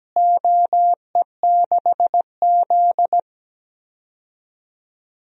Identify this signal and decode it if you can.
Morse code, heard as a keyed tone, about 17 wpm, 700 Hz. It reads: OE6Z
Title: DXCC-100 Call Signs - 4 Characters - Repeated 3x - 17wpm